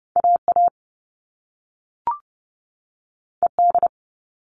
Morse code, heard as a keyed tone, about 30 wpm, 700 Hz. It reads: AA ED
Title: All Cut Numbers - 2-Digit 30wpm